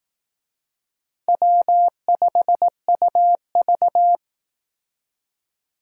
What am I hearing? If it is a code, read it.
Morse code, heard as a keyed tone, about 18 wpm, 700 Hz. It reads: W5UV
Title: Speed Racing - USA 1x2 Call Signs 15wpm